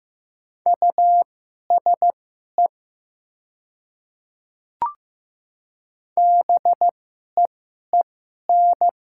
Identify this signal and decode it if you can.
Morse code, heard as a keyed tone, about 15 wpm, spaced as for 11 wpm, 700 Hz. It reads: USE BEEN
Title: 2x Character Spacing - Top 100 Words - 15wpm